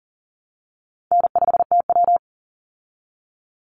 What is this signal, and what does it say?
Morse code, heard as a keyed tone, about 40 wpm, 700 Hz. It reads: N5TW
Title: Easy Call Signs Repeated 2x - 40wpm